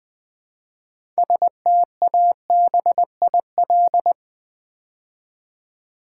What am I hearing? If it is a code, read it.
Morse code, heard as a keyed tone, about 20 wpm, 700 Hz. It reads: STABIL
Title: Build A Word - 3 Letters At A Time - 9-Letter Words - 20wpm